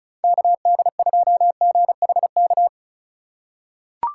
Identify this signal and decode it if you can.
Morse code, heard as a keyed tone, about 36 wpm, 700 Hz. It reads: KD2GHK
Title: USA Call Signs - 3x Character Spacing - 35wpm